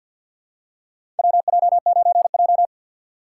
Morse code, heard as a keyed tone, about 53 wpm, 700 Hz. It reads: WJ9J
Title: USA Call Signs - 4 Characters - 6x Character Spacing - 50wpm